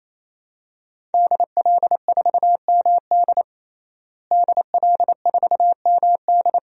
Morse code, heard as a keyed tone, about 27 wpm, 700 Hz. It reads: DL4MD DL4MD
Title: DXCC-100 Call Signs - 5 Characters - Repeated 3x - 28wpm